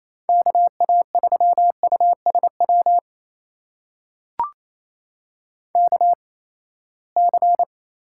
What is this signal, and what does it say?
Morse code, heard as a keyed tone, about 28 wpm, spaced as for 19 wpm, 700 Hz. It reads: KA3USW KC
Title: USA Call Signs - 6 Characters - 8x Character Spacing - 28wpm